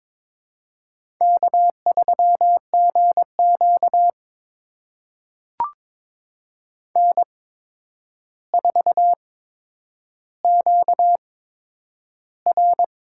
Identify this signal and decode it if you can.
Morse code, heard as a keyed tone, about 22 wpm, spaced as for 9 wpm, 700 Hz. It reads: K3GQ N4QR
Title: USA Call Signs - 4 Characters - 8x Character Spacing - 22wpm